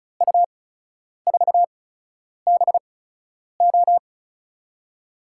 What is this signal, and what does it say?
Morse code, heard as a keyed tone, about 34 wpm, spaced as for 10 wpm, 700 Hz. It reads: U4BO
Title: USA Call Signs - 5 Characters - 8x Character Spacing - 35wpm